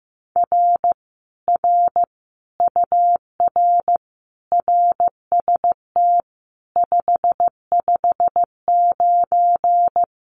Morse code, heard as a keyed tone, about 15 wpm, 700 Hz. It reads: R R UR RST 559
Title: Common QSO Phrases Encore 15wpm